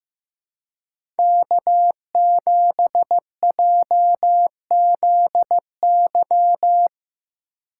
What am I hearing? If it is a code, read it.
Morse code, heard as a keyed tone, about 15 wpm, 700 Hz. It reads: K7JZY
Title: USA Call Signs - 2x Character Spacing - 15wpm